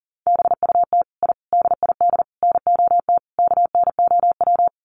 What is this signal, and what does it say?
Morse code, heard as a keyed tone, about 40 wpm, 700 Hz. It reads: BUT I DID NOT KNOW
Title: Sets of 5 Words Encore 40wpm